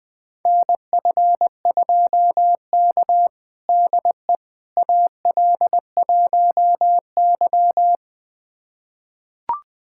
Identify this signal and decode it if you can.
Morse code, heard as a keyed tone, about 21 wpm, 700 Hz. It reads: NF2K DE AL1Y
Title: USA Callsign DE Callsign - 4 Characters - 20wpm